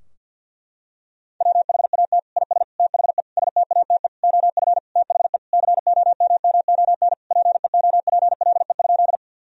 Morse code, heard as a keyed tone, about 50 wpm, 700 Hz. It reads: WHAT IS THE STATE OF THE COMMON PEOPLE?
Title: Sentences from Top 1000 Words - Review Part 3 - 50wpm